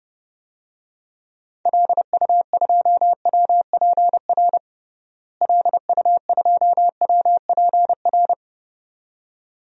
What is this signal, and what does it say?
Morse code, heard as a keyed tone, about 31 wpm, 700 Hz. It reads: LU2WPR LU2WPR
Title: DXCC-100 Call Signs - 6 Characters - Repeated 2x - 30wpm